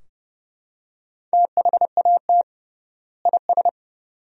Morse code, heard as a keyed tone, about 32 wpm, 700 Hz. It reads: THAT IS
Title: 3x Word-Spacing - Sentences from Top 400 Words - 30wpm